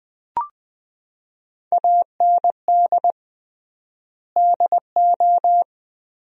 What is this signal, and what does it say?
Morse code, heard as a keyed tone, about 20 wpm, 700 Hz. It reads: AND DO
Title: 3x Word-Spacing - Sentences from Top 100 Words - 20wpm